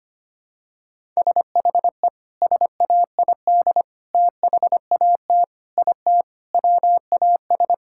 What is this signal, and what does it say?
Morse code, heard as a keyed tone, about 25 wpm, 700 Hz. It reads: SHE SAID THAT IT WAS
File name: practice / 25wpm / Sentences with 3-Word Phrases - 25wpm